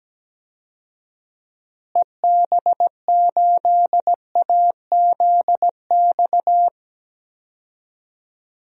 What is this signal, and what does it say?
Morse code, heard as a keyed tone, about 17 wpm, 700 Hz. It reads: EB8AZX
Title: DXCC-100 Call Signs - 6 Characters - Repeated 2x - 17wpm